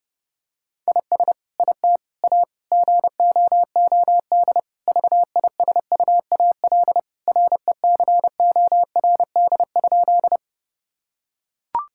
Morse code, heard as a keyed tone, about 30 wpm, 700 Hz. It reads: IS IT A GOOD VISUAL RECORD?
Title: Sentences from Top 1000 Words - Review Part 6 - 30wpm